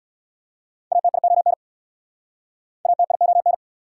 Morse code, heard as a keyed tone, about 59 wpm, 690 Hz. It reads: ATION ATION
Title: Speed-Racing Prefixes and Suffixes 50wpm